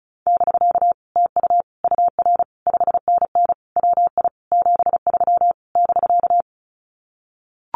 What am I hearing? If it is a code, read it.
Morse code, heard as a keyed tone, about 34 wpm, 700 Hz. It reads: <BK> TU UR 5NN WI 73 <BK>
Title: Common QSO Phrases Encore 35wpm